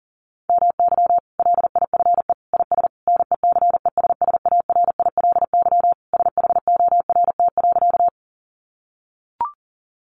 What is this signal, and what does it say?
Morse code, heard as a keyed tone, about 40 wpm, 700 Hz. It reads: MY LIFE IS NECESSARILY SHORT.
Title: Sentences from Top 1400 Words 40wpm